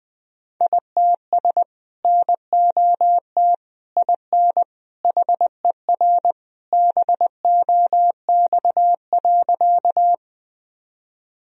Morse code, heard as a keyed tone, about 21 wpm, 700 Hz. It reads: ITS NOT IN HER BOX.
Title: Sentences from Top 1000 Words - Review Part 2 - 20wpm